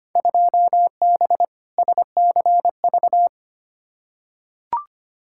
Morse code, heard as a keyed tone, about 25 wpm, 700 Hz. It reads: 2B SCV
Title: Field Day Exchanges Encore - 25wpm